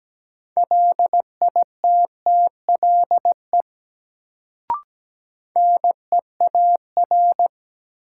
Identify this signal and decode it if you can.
Morse code, heard as a keyed tone, about 17 wpm, 700 Hz. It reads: LITTLE NEAR
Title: Sentences from Top 200 Words - New Word Drill - 17wpm